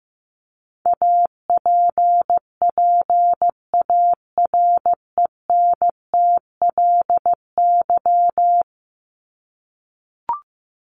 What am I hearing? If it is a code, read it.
Morse code, heard as a keyed tone, about 15 wpm, 700 Hz. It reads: APPARENTLY
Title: LY Suffix - Top 100 Words with Base Words 15wpm